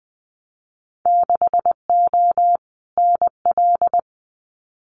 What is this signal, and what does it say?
Morse code, heard as a keyed tone, about 20 wpm, 700 Hz. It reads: 6O NL
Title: Winter Field Day Exchanges Encore 20wpm